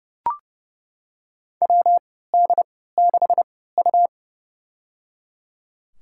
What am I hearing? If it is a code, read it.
Morse code, heard as a keyed tone, about 30 wpm, spaced as for 17 wpm, 700 Hz. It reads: WD6U
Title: USA Call Signs - 4 Characters - 3x Character Spacing - 30wpm